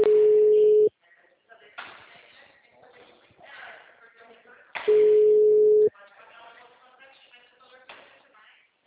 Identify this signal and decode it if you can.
Ringback tone (Europe, 425 Hz), with speech.